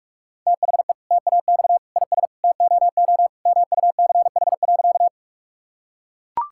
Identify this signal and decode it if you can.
Morse code, heard as a keyed tone, about 43 wpm, 700 Hz. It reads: THE TAX IS TOO MUCH.
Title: Sentences from Top 1000 Words - Review Part 5 - 45wpm